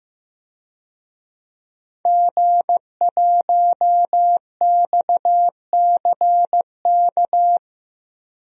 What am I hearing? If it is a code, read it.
Morse code, heard as a keyed tone, about 15 wpm, 700 Hz. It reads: G1XCK
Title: DXCC-100 Call Signs - 5 Characters - 15wpm